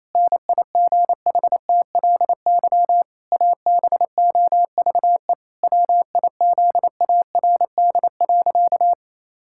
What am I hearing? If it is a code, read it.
Morse code, heard as a keyed tone, about 28 wpm, 700 Hz. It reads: NIGHTLY ABOVE WIZARD.
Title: Pangram Challenge v2 - 28wpm